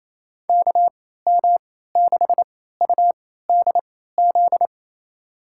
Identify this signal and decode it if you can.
Morse code, heard as a keyed tone, about 27 wpm, spaced as for 16 wpm, 700 Hz. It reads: KM6UDZ
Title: USA Call Signs - 6 Characters - 3x Character Spacing - 28wpm